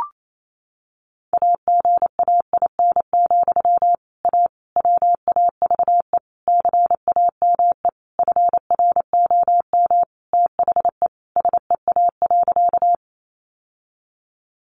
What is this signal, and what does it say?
Morse code, heard as a keyed tone, about 28 wpm, 700 Hz. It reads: AGAIN, A WAVE CAME FROM THE SEA.
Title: Sentences from Top 1400 Words 28wpm